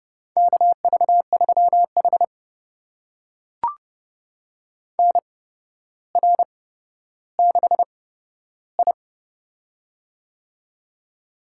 Morse code, heard as a keyed tone, about 31 wpm, spaced as for 13 wpm, 700 Hz. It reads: KV3H NR6I
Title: USA Call Signs - 4 Characters - 8x Character Spacing - 30wpm